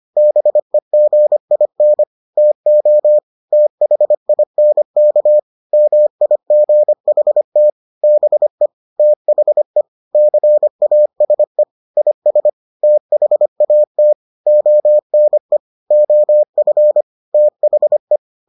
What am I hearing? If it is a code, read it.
Morse code, heard as a keyed tone, about 25 wpm, 600 Hz. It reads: BEGIN TO THINK MIGHT BE THE CASE IS THAT ONE OF THE